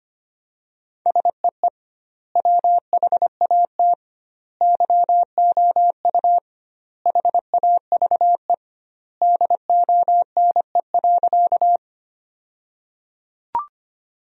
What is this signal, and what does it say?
Morse code, heard as a keyed tone, about 25 wpm, 700 Hz. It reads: SEE WHAT YOU HAVE DONE.
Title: 2x Word-Spacing - Sentences from Top 500 Words - 25wpm